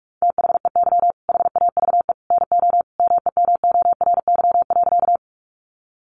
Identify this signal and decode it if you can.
Morse code, heard as a keyed tone, about 44 wpm, 700 Hz. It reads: THEY HAVE NO MEMORY.
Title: Sentences from Top 1300 Words 45wpm